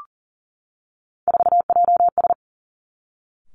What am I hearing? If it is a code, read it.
Morse code, heard as a keyed tone, about 40 wpm, 700 Hz. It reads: 4JS
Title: Three Challenging Letter-Number Combinations 40wpm